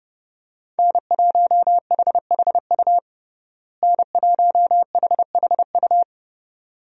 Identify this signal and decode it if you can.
Morse code, heard as a keyed tone, about 30 wpm, 700 Hz. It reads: N1HHU N1HHU
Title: USA Call Signs - 5 Characters - Repeated 2x - 30wpm